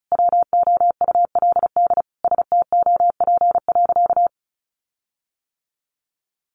Morse code, heard as a keyed tone, about 34 wpm, 700 Hz. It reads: WOULD STOP.
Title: Sentences from Top 1300 Words 35wpm